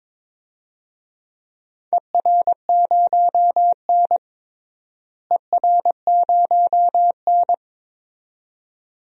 Morse code, heard as a keyed tone, about 22 wpm, 700 Hz. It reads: ER0N ER0N
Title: DXCC-100 Call Signs - 4 Characters - Repeated 2x - 22wpm